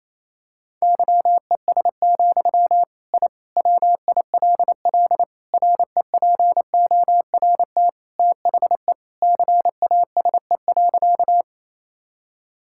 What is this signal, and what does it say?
Morse code, heard as a keyed tone, about 28 wpm, 700 Hz. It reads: YES, I WILL REPORT THE CASE.
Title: Sentences from Top 1000 Words - Review Part 6 - 28wpm